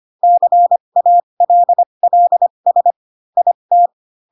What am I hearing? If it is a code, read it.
Morse code, heard as a keyed tone, about 25 wpm, spaced as for 22 wpm, 700 Hz. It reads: CALLS IT
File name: Pinocchio / Pinocchio ch3 18 25 ik1hge